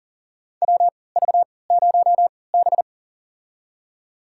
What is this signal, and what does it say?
Morse code, heard as a keyed tone, about 40 wpm, spaced as for 23 wpm, 700 Hz. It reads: WV0B
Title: USA Call Signs - 4 Characters - 3x Character Spacing - 40wpm